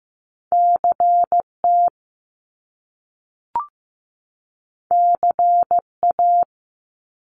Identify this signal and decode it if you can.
Morse code, heard as a keyed tone, about 15 wpm, 700 Hz. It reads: CT CA
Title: US States 15wpm